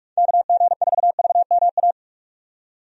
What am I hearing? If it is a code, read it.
Morse code, heard as a keyed tone, about 43 wpm, 700 Hz. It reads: KG4VMA
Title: USA Call Signs - 6 Characters - 45wpm